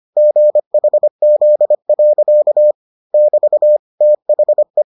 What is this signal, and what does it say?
Morse code, heard as a keyed tone, about 25 wpm, 600 Hz. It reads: GHZ. <BT> THE